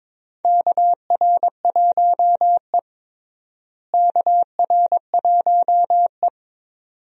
Morse code, heard as a keyed tone, about 22 wpm, 700 Hz. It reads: KR1E KR1E
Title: USA Call Signs - 4 Characters - Repeated 2x - 22wpm